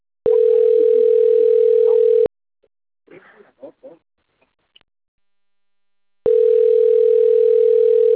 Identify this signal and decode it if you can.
Ringback tone (North America), 440 + 480 Hz.